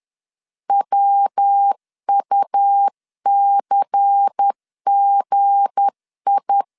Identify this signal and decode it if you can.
Morse code, heard as a keyed tone, about 10 wpm, 800 Hz. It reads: WUCGI
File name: LEZIONI / 9 286 10wpm 2735